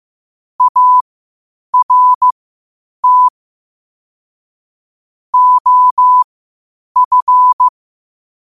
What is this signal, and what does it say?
Morse code, heard as a keyed tone, about 15 wpm, spaced as for 8 wpm, 1000 Hz. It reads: ART OF
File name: qotdfiles / 8wpm